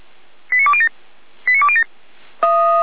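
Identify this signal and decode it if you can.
Five-tone selective call (ZVEI), heard as quick runs of five tones, tones of about 70 ms. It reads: 89207 89207
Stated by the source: ZVEI five-tone selective call received by SDR, fire brigade alerting, from feuerwehr-orlamuende.org.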